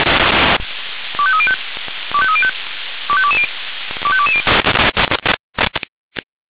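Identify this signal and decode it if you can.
Five-tone selective call (ZVEI), heard as quick runs of five tones, tones of about 70 ms. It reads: 25305 25305 25220 25220
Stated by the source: ZVEI five-tone selective call received by SDR.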